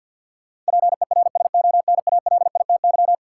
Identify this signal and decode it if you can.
Morse code, heard as a keyed tone, about 53 wpm, 700 Hz. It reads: PERSONALITY
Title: Sentences from Top 1400 Words - New Word Drill - 50wpm